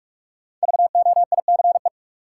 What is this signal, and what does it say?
Morse code, heard as a keyed tone, about 45 wpm, 700 Hz. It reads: VOICE
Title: Sentences from Top 500 Words - New Word Drill - 45wpm